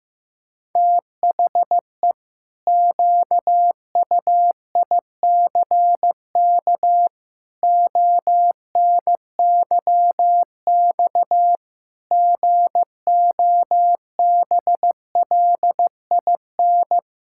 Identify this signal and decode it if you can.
Morse code, heard as a keyed tone, about 15 wpm, 700 Hz. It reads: THE QUICK ONYX GOBLIN
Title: Pangram Challenge v2 - 15wpm